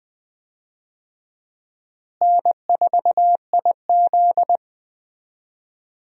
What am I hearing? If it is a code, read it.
Morse code, heard as a keyed tone, about 19 wpm, 700 Hz. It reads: N4IZ
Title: USA Call Signs - 4 Characters - Repeated 2x - 20wpm